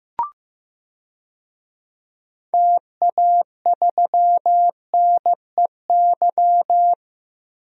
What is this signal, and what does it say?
Morse code, heard as a keyed tone, about 15 wpm, 700 Hz. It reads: TA3NEY
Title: DXCC-100 Call Signs - 6 Characters - 15wpm